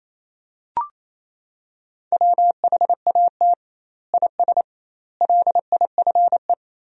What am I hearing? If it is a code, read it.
Morse code, heard as a keyed tone, about 28 wpm, 700 Hz. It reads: WHAT IS LIFE
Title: 2x Word-Spacing - Sentences from Top 300 Words - 28wpm